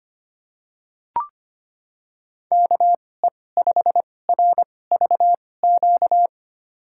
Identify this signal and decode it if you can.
Morse code, heard as a keyed tone, about 25 wpm, spaced as for 18 wpm, 700 Hz. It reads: KE5RVQ
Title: USA Call Signs - 6 Characters - 2x Character Spacing - 25wpm